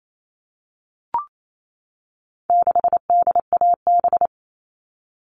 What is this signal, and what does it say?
Morse code, heard as a keyed tone, about 28 wpm, 700 Hz. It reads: 6DAB
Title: All Cut Numbers - 4-Digit 28wpm